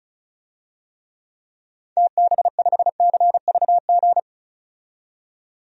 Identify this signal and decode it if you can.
Morse code, heard as a keyed tone, about 34 wpm, 700 Hz. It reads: TB5CVG
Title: DXCC-100 Call Signs - 6 Characters - 35wpm